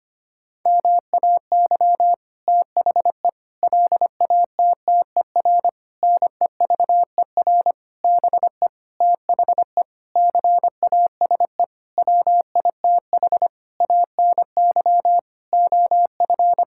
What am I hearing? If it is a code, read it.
Morse code, heard as a keyed tone, about 25 wpm, 700 Hz. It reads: MAY THE LATTER NEVER BE THE CASE WITH ANY OF
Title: Sentences from Top 1600 Words 25wpm